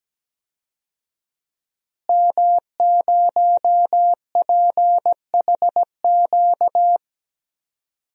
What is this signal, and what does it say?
Morse code, heard as a keyed tone, about 17 wpm, 700 Hz. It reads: M0PHQ
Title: DXCC-100 Call Signs - 5 Characters - Repeated 3x - 17wpm